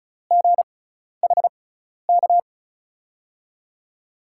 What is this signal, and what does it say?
Morse code, heard as a keyed tone, about 36 wpm, spaced as for 12 wpm, 700 Hz. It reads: GHK
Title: USA Call Signs - 6 Characters - 6x Character Spacing - 35wpm